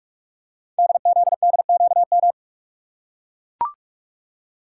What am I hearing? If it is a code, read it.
Morse code, heard as a keyed tone, about 44 wpm, 690 Hz. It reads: DZDQM
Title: Code Groups - 5 Letters - 45wpm